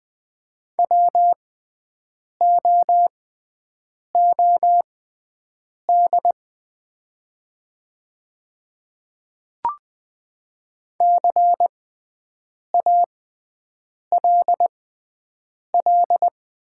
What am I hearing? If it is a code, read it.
Morse code, heard as a keyed tone, about 20 wpm, spaced as for 7 wpm, 700 Hz. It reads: WOOD CALL
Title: 6x Character Spacing - Top 300 Words - 20wpm